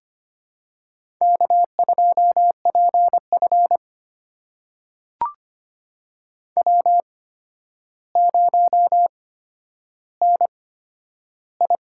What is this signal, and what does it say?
Morse code, heard as a keyed tone, about 25 wpm, spaced as for 11 wpm, 700 Hz. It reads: K2PF W0NI
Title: USA Call Signs - 4 Characters - 8x Character Spacing - 25wpm